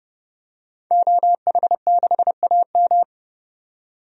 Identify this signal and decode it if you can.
Morse code, heard as a keyed tone, about 30 wpm, 700 Hz. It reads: OH6AM
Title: DXCC-100 Call Signs - 5 Characters - 30wpm